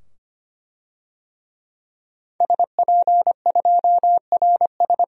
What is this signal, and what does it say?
Morse code, heard as a keyed tone, about 25 wpm, 700 Hz. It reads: SP2RS
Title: DXCC-100 Call Signs - 5 Characters - 25wpm